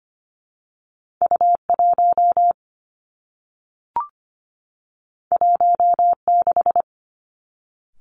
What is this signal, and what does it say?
Morse code, heard as a keyed tone, about 25 wpm, 700 Hz. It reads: U 1 1 6
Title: Two Challenging Letter-Number Combinations 25wpm